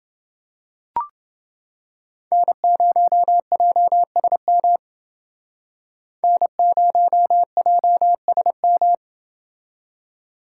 Morse code, heard as a keyed tone, about 28 wpm, 700 Hz. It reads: N0JSM N0JSM